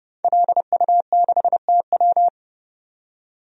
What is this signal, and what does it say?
Morse code, heard as a keyed tone, about 30 wpm, 700 Hz. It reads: LU6TW
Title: DXCC-100 Call Signs - 5 Characters - 30wpm